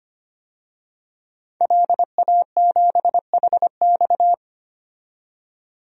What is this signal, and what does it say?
Morse code, heard as a keyed tone, about 25 wpm, 700 Hz. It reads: LA7HX
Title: DXCC-100 Call Signs - 5 Characters - 25wpm